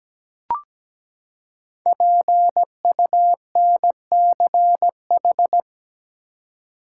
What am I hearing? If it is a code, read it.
Morse code, heard as a keyed tone, about 17 wpm, 700 Hz. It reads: PUNCH
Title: Five Letter Words - 17wpm